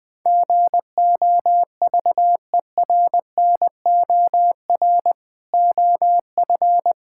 Morse code, heard as a keyed tone, about 20 wpm, 700 Hz. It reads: GOVERNOR OF
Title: Sentences with 3-Word Phrases - 20wpm